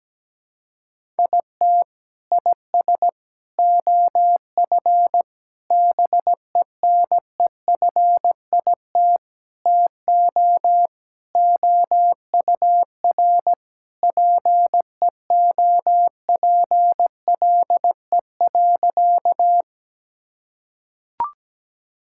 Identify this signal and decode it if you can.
Morse code, heard as a keyed tone, about 17 wpm, 700 Hz. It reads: IT IS OF BENEFIT TO OUR PEOPLE.
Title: Sentences from Top 1000 Words - Review Part 6 - 17wpm